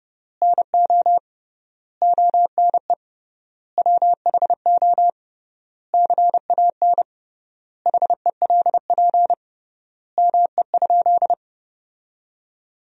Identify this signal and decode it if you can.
Morse code, heard as a keyed tone, about 30 wpm, 700 Hz. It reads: NO ONE WHO CAN HELP ME?
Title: 3x Word-Spacing - Sentences from Top 200 Words - 30wpm